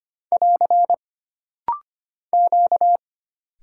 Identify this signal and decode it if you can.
Morse code, heard as a keyed tone, about 25 wpm, 700 Hz. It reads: <AR> Q
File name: Single Character (Letter:Number:Punctuation:Prosign) - Rapid-Fire - 25wpm